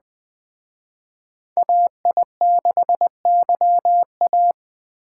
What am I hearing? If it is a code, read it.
Morse code, heard as a keyed tone, about 24 wpm, 700 Hz. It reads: AI6YA
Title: Speed Racing - USA Call Signs 20wpm